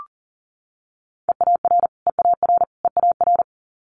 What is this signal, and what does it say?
Morse code, heard as a keyed tone, about 39 wpm, 700 Hz. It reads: EAR EAR EAR
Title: Multi-letter Phonemes Repeated 3x 40wpm